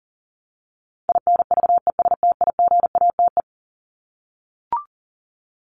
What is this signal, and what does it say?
Morse code, heard as a keyed tone, about 40 wpm, 700 Hz. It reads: INVESTIGATE
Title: Top 1751-2000 Words 40wpm